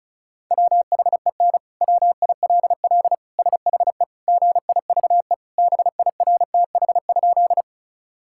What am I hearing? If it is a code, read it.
Morse code, heard as a keyed tone, about 34 wpm, 700 Hz. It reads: WHEN WILL SHE GIVE BIRTH?
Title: Sentences from Top 1000 Words - Review Part 5 - 35wpm